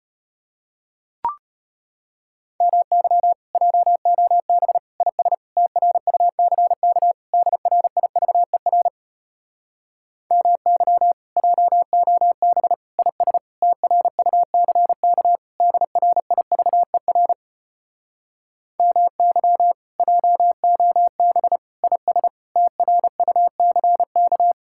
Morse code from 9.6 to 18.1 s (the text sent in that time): MY JOB IS TRUCK DRIVER